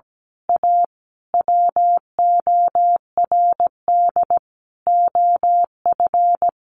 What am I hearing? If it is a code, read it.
Morse code, heard as a keyed tone, about 17 wpm, 700 Hz. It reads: A WORD OF
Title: Sentences from Top 1200 Words 17wpm